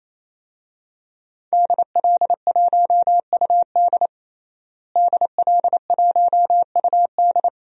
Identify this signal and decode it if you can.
Morse code, heard as a keyed tone, about 27 wpm, 700 Hz. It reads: DL1UD DL1UD
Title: DXCC-100 Call Signs - 5 Characters - Repeated 3x - 28wpm